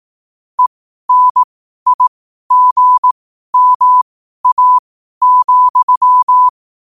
Morse code, heard as a keyed tone, about 18 wpm, spaced as for 12 wpm, 1000 Hz. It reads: ENIGMA,
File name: qotdfiles / 12wpm